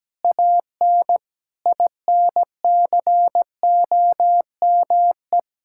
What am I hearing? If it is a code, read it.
Morse code, heard as a keyed tone, about 17 wpm, 700 Hz. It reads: AN INCOME
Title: Sentences from Top 1000 Words - Review Part 5 - 17wpm